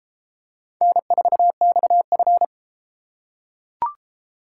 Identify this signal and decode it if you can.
Morse code, heard as a keyed tone, about 32 wpm, 700 Hz. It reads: N4XF